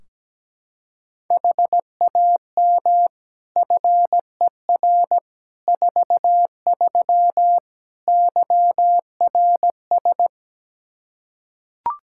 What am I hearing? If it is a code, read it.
Morse code, heard as a keyed tone, about 17 wpm, 700 Hz. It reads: HAM FER 43 YRS
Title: Practice QSO - Exchange 3 - More Info - 17wpm